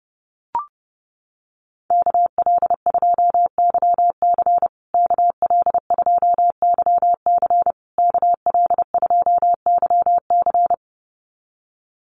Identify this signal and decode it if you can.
Morse code, heard as a keyed tone, about 30 wpm, 700 Hz. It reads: KL2YC KL2YC KL2YC
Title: USA Call Signs Repeated 3x 30wpm